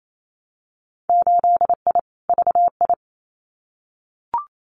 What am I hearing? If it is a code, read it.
Morse code, heard as a keyed tone, about 28 wpm, 700 Hz. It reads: 8I VI
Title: Winter Field Day Exchanges Encore 28wpm